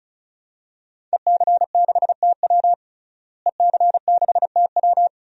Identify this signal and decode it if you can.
Morse code, heard as a keyed tone, about 34 wpm, 700 Hz. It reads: EC6TW EC6TW
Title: DXCC-100 Call Signs - 5 Characters - Repeated 2x - 35wpm